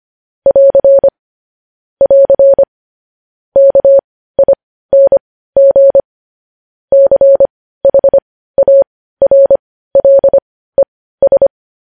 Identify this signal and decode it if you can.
Morse code, heard as a keyed tone, about 25 wpm, spaced as for 15 wpm, 550 Hz. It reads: <AR> <AR> KING CHARLES
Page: News Headlines in Morse Code at 15 WPM